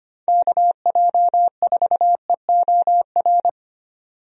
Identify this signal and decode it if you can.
Morse code, heard as a keyed tone, about 25 wpm, 700 Hz. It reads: KJ4EOR
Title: USA Call Signs - 6 Characters - 25wpm